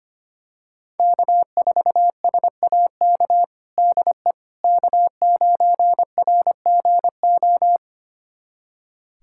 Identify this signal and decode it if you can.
Morse code, heard as a keyed tone, about 25 wpm, 700 Hz. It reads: K4SAK DE K9RGO
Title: USA Callsign DE Callsign - 5 Characters - 25wpm